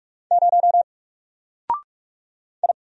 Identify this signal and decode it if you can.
Morse code, heard as a keyed tone, about 46 wpm, 700 Hz. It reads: 0 I
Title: Single Letter-Number Alt 45wpm